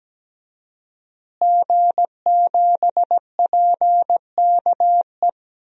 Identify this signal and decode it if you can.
Morse code, heard as a keyed tone, about 17 wpm, 700 Hz. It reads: G7PKE
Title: DXCC-100 Call Signs - 5 Characters - Repeated 3x - 17wpm